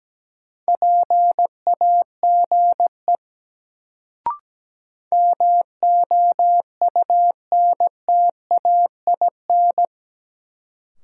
Morse code, heard as a keyed tone, about 17 wpm, 700 Hz. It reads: PAGE MOUNTAIN
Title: Sentences from Top 1000 Words - Review Part 1 - New Word Drill - 17wpm